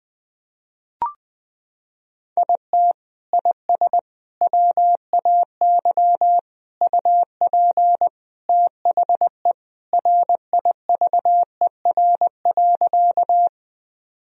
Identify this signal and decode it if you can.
Morse code, heard as a keyed tone, about 20 wpm, 700 Hz. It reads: IT IS WAY UP THE RIVER.
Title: Sentences from Top 1000 Words - Review Part 1 - 20wpm